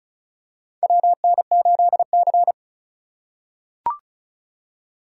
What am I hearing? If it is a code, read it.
Morse code, heard as a keyed tone, about 36 wpm, 700 Hz. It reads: WN8C
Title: USA Call Signs - 6x Character Spacing - 35wpm